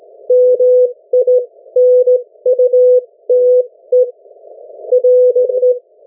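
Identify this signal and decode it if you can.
Morse code, heard as a keyed tone, about 17 wpm, spaced as for 15 wpm, 500 Hz. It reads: MINUTE RE
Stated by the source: Morse code from the national amateur radio CW net of New Zealand, which meets on 3535.0 kHz or 7030 kHz.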